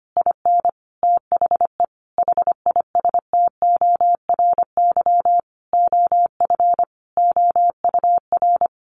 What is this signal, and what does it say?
Morse code, heard as a keyed tone, about 25 wpm, 700 Hz. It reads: IN THE HISTORY OF OUR
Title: Sentences from Top 1500 Words 25wpm